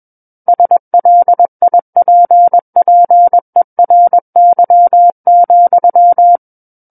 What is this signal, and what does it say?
Morse code, heard as a keyed tone, about 21 wpm, 700 Hz. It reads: SLIPPERY,
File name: Book One / 12-Chapter 12 - 21 WPM